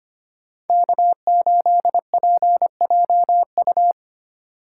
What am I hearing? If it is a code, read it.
Morse code, heard as a keyed tone, about 30 wpm, 700 Hz. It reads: K8PJU
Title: Speed Racing - USA 1x3 Call Signs 25wpm